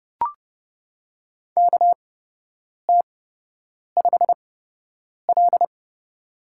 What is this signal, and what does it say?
Morse code, heard as a keyed tone, about 30 wpm, spaced as for 8 wpm, 700 Hz. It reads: KT5L
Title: USA Call Signs - 4 Characters - 8x Character Spacing - 30wpm